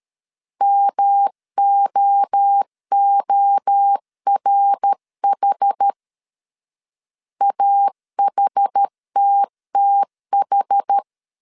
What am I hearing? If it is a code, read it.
Morse code, heard as a keyed tone, about 13 wpm, 800 Hz. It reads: MOORH AHTTH
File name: CW / DET / 1 3 1500C S A 12wpm 2905